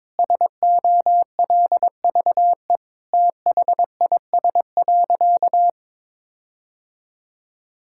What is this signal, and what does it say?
Morse code, heard as a keyed tone, about 22 wpm, 700 Hz. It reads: SOLVE THIS.
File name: practice / 22wpm / Sentences with 4-Word Phrases - 22wpm